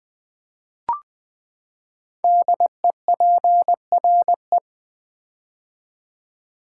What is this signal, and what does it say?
Morse code, heard as a keyed tone, about 20 wpm, 700 Hz. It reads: DEPRE
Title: Build A Word - 5 Letters At A Time - 10-Letter Words - 20wpm